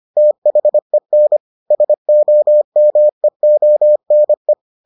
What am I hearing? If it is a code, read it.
Morse code, heard as a keyed tone, about 25 wpm, 600 Hz. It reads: THEN SOMEONE